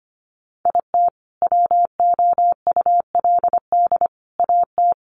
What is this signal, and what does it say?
Morse code, heard as a keyed tone, about 25 wpm, 700 Hz. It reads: IT WOULD AT
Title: Sentences from Top 700 Words 25wpm